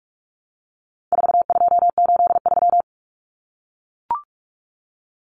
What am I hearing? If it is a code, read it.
Morse code, heard as a keyed tone, about 44 wpm, 700 Hz. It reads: 4283
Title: Common Cut Numbers - 4-Digit 45wpm